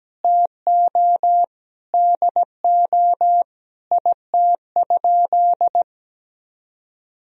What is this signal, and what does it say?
Morse code, heard as a keyed tone, about 17 wpm, 700 Hz. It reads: TO DO IT?
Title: Sentences from Top 1000 Words - Review Part 2 - 17wpm